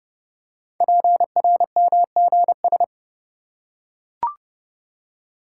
Morse code, heard as a keyed tone, about 30 wpm, 690 Hz. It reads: PRMGS
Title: Code Groups - 5 Letters - 30wpm